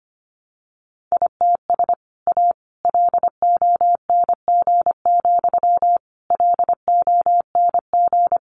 Morse code, heard as a keyed tone, about 25 wpm, 700 Hz. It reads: ITS A LONG, LONG
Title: Sentences from Top 600 Words 25wpm